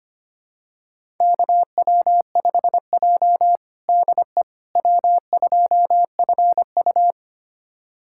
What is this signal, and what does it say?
Morse code, heard as a keyed tone, about 25 wpm, 700 Hz. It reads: KW5J DE W2FU
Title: USA Callsign DE Callsign - 4 Characters - 25wpm